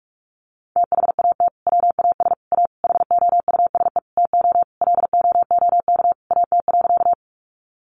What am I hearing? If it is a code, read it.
Morse code, heard as a keyed tone, about 44 wpm, 700 Hz. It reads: THAT WAS A HOUSE TO LOOK AT.
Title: Sentences from Top 200 Words 45wpm